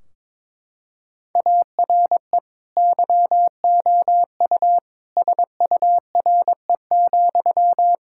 Morse code, heard as a keyed tone, about 21 wpm, 700 Hz. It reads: ARE YOU SURE,
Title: Sentences from Top 1600 Words 22wpm